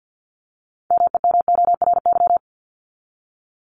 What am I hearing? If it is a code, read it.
Morse code, heard as a keyed tone, about 50 wpm, 700 Hz. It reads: MEMORY
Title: Top 1251-1500 Words 50wpm